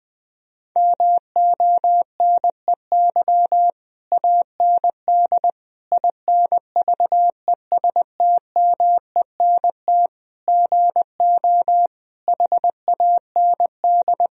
Morse code, heard as a keyed tone, about 20 wpm, 700 Hz. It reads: MONEY AND INVESTMENT GO HAND
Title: Sentences from Top 1000 Words - Review Part 4 - 20wpm